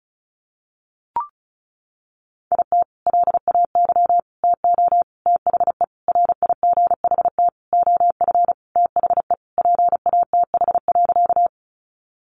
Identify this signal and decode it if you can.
Morse code, heard as a keyed tone, about 34 wpm, 700 Hz. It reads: IT LAY TO THE RIGHT OF THE PATH.
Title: Sentences from Top 1400 Words 35wpm